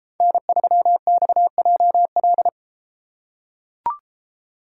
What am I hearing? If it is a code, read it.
Morse code, heard as a keyed tone, about 32 wpm, 700 Hz. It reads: N3XJL